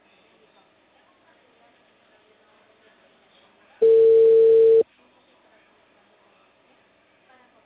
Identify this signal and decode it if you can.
Ringback tone (China, 450 Hz), with speech.